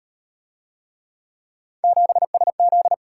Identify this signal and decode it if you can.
Morse code, heard as a keyed tone, about 38 wpm, 700 Hz. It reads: 7SZ